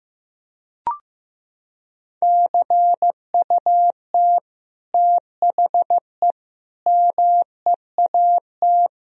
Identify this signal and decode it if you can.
Morse code, heard as a keyed tone, about 15 wpm, 700 Hz. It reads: CUT THE MEAT
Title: Sentences from Top 1000 Words - Review Part 3 - 15wpm